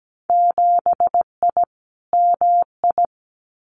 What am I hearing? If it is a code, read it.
Morse code, heard as a keyed tone, about 17 wpm, 700 Hz. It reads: 7I MI
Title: Winter Field Day Exchanges Encore 17wpm